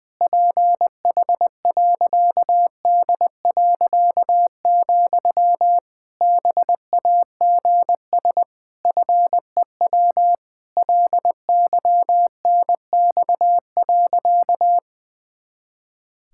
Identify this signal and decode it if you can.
Morse code, heard as a keyed tone, about 21 wpm, 700 Hz. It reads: PH.D., BAGS FEW LYNX.
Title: Pangram Challenge v2 - 20wpm